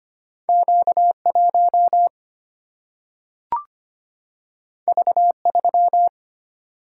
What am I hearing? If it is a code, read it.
Morse code, heard as a keyed tone, about 25 wpm, 700 Hz. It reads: Q 1 V 3
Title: Single-Letter Single-Number 25wpm